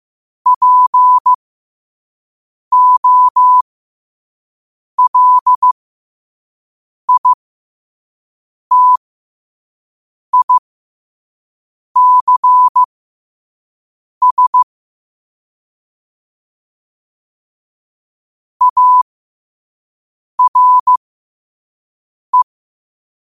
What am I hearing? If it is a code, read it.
Morse code, heard as a keyed tone, about 15 wpm, spaced as for 5 wpm, 1000 Hz. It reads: POLITICS ARE
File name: qotdfiles / 5wpm